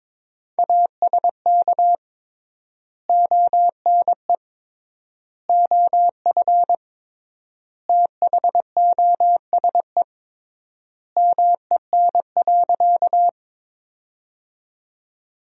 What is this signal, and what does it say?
Morse code, heard as a keyed tone, about 22 wpm, 700 Hz. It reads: ASK ONE OF THOSE MEN.
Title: 3x Word-Spacing - Sentences from Top 300 Words - 22wpm